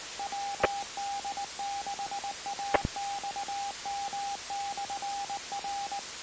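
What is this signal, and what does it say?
Morse code, heard as a keyed tone, about 19 wpm, 800 Hz, with radio noise. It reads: WD6AXM/R
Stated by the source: Morse code received on 2 m, 146.085 MHz.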